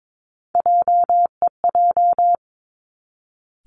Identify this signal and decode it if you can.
Morse code, heard as a keyed tone, about 22 wpm, 700 Hz. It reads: JEJ
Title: Three Challenging Letter-Number Combinations 22wpm